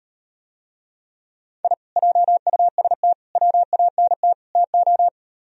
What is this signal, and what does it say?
Morse code, heard as a keyed tone, about 38 wpm, 700 Hz. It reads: I JUST WANT TO